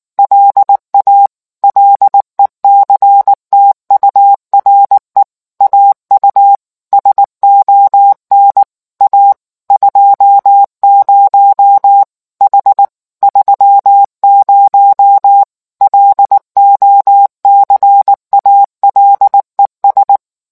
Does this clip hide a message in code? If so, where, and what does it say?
Yes: Morse code from 0.2 to 20.2 s, heard as a keyed tone, about 19 wpm, 800 Hz. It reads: LA LECTURE AU SON A 20 H 30 LOCALES